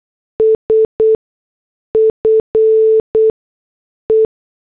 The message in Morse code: SFE